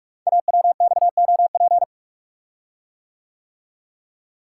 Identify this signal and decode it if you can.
Morse code, heard as a keyed tone, about 44 wpm, 700 Hz. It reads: AWXOP
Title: Code Groups - 5 Letters - 45wpm